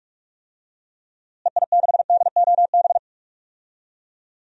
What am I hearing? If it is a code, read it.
Morse code, heard as a keyed tone, about 46 wpm, 700 Hz. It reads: EI6DOB
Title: DXCC-100 Call Signs - 6 Characters - Repeated 3x - 45wpm